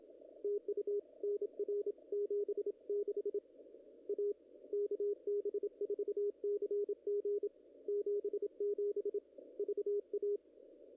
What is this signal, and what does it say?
Morse code, heard as a keyed tone, about 27 wpm, 410 Hz. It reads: TU NR 76 A KB4CG 77 VA
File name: contests / ss-cw-k5tr-op-n5rz -0000